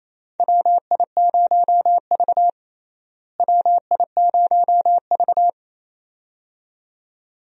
Morse code, heard as a keyed tone, about 28 wpm, 700 Hz. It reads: WI0V WI0V
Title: USA Call Signs - 4 Characters - Repeated 2x - 28wpm